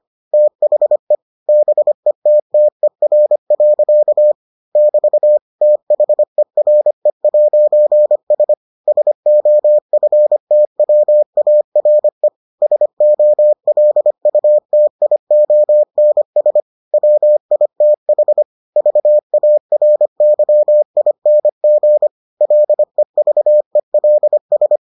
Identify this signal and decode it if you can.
Morse code, heard as a keyed tone, about 25 wpm, 600 Hz. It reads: THE BETTER. <BT> THERE'S SOFTWARE SOLUTIONS WITH VARYING LEVELS